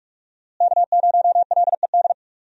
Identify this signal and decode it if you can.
Morse code, heard as a keyed tone, about 48 wpm, 700 Hz. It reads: K0LED
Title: USA Call Signs - 5 Characters - Repeated 2x - 45wpm